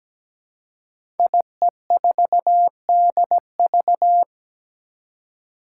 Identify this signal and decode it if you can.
Morse code, heard as a keyed tone, about 17 wpm, 700 Hz. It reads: IE4DV
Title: Five Challenging Letter-Number Combinations 17wpm